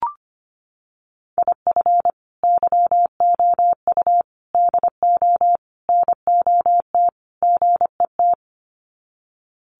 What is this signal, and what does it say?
Morse code, heard as a keyed tone, about 25 wpm, 700 Hz. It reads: IF YOU DO NOT GET
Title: Sets of 5 Words Encore2 25wpm